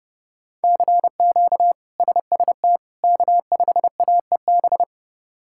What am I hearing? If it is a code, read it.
Morse code, heard as a keyed tone, about 30 wpm, 700 Hz. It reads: CQ SST K5AEB
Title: K1USN SST QSO Practice - 30wpm